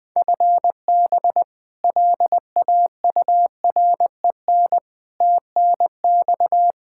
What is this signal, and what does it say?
Morse code, heard as a keyed tone, about 20 wpm, 700 Hz. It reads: FB LAUREN TNX
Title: K1USN SST QSO Practice 20wpm